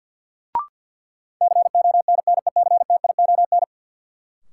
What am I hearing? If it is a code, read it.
Morse code, heard as a keyed tone, about 46 wpm, 700 Hz. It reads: CONNECTION
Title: Sentences from Top 1300 Words - New Word Drill - 50wpm